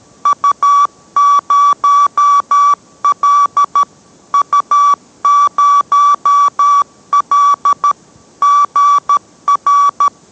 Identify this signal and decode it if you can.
Morse code, heard as a keyed tone, about 13 wpm, 1150 Hz. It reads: U0L U0L GR